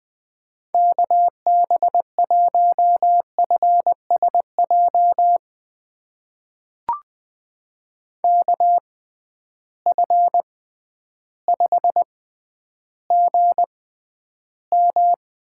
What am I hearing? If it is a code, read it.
Morse code, heard as a keyed tone, about 20 wpm, spaced as for 11 wpm, 700 Hz. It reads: KB1FSJ KF5GM
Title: USA Call Signs - 6 Characters - 6x Character Spacing - 20wpm